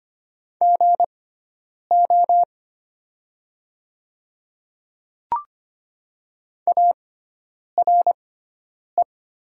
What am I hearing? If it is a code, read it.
Morse code, heard as a keyed tone, about 25 wpm, spaced as for 9 wpm, 700 Hz. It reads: GO ARE
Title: 6x Character Spacing - Top 100 Words - 25wpm